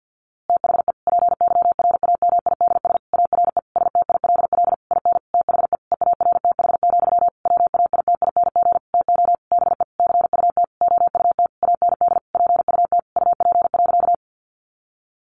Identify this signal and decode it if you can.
Morse code, heard as a keyed tone, about 50 wpm, 700 Hz. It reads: THE PYRAMIDS ARE STILL IN THE EARTH, WAITING TO BE CUT OUT AND PUT UP.
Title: A Sound Of Thunder 50wpm